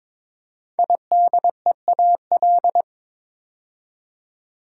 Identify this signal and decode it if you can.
Morse code, heard as a keyed tone, about 22 wpm, 700 Hz. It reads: IDEAL
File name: practice / 22wpm / 5 Letter Words - First 3 Letters Given - 22wpm